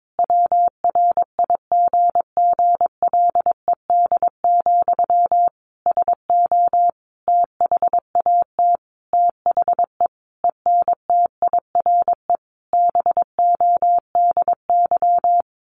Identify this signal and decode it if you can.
Morse code, heard as a keyed tone, about 22 wpm, 700 Hz. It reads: WRIGGLED, SO THAT THE ENTIRE BODY
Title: A Sound Of Thunder 22wpm